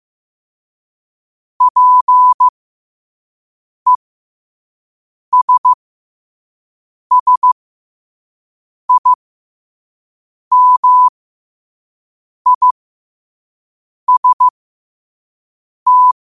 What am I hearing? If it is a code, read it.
Morse code, heard as a keyed tone, about 15 wpm, spaced as for 5 wpm, 1000 Hz. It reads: PESSIMIST
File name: qotdfiles / 5wpm